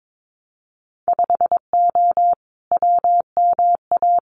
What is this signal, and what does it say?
Morse code, heard as a keyed tone, about 22 wpm, 700 Hz. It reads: 5O WMA
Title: Winter Field Day Exchanges Encore 22wpm